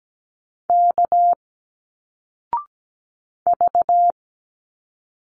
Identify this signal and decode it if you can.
Morse code, heard as a keyed tone, about 17 wpm, 700 Hz. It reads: K V
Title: Single Letters 17wpm